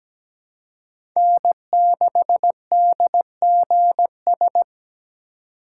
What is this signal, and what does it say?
Morse code, heard as a keyed tone, about 17 wpm, 700 Hz. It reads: N6DGS
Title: USA Call Signs - 4x Character Spacing - 17wpm